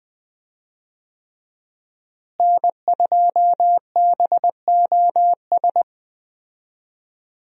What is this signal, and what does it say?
Morse code, heard as a keyed tone, about 20 wpm, 700 Hz. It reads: N2BOS
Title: USA Call Signs - 5 Characters - Repeated 3x - 20wpm